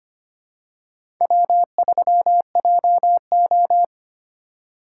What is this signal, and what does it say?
Morse code, heard as a keyed tone, about 25 wpm, 700 Hz. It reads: W3JO
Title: USA Call Signs - 4 Characters - 6x Character Spacing - 25wpm